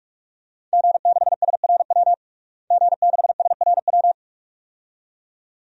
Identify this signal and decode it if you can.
Morse code, heard as a keyed tone, about 46 wpm, 700 Hz. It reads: G6SRW G6SRW
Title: DXCC-100 Call Signs - 5 Characters - Repeated 2x - 45wpm